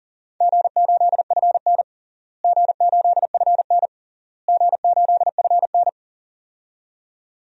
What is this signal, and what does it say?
Morse code, heard as a keyed tone, about 40 wpm, 700 Hz. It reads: G8FN G8FN G8FN
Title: DXCC-100 Call Signs - 4 Characters - Repeated 3x - 40wpm